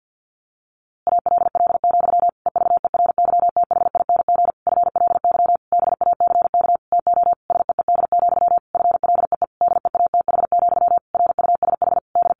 Morse code, heard as a keyed tone, about 50 wpm, 700 Hz. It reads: ALL, EVERYTHING FLY BACK TO SEED, FLEE DEATH, RUSH D
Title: A Sound Of Thunder 50wpm